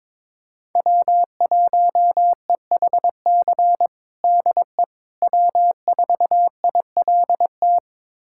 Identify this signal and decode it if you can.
Morse code, heard as a keyed tone, about 22 wpm, 700 Hz. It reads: W1EHC DE W4ILT
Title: USA Callsign DE Callsign - 5 Characters - 22wpm